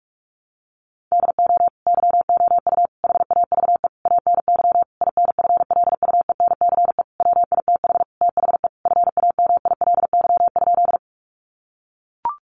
Morse code, heard as a keyed tone, about 44 wpm, 700 Hz. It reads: DO YOU HAVE ANY INFLUENCE WITH THE FAMILY?
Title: Sentences from Top 1100 Words 45wpm